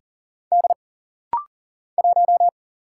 Morse code, heard as a keyed tone, about 40 wpm, 700 Hz. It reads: D 1
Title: Single Letter-Number - Rapid-Fire 40wpm